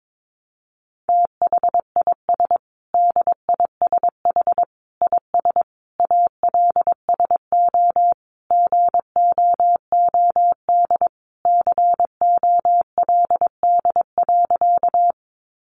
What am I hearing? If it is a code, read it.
Morse code, heard as a keyed tone, about 22 wpm, 700 Hz. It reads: THIS DISH IS ALSO GOOD COLD.
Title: Sentences from Top 1300 Words 22wpm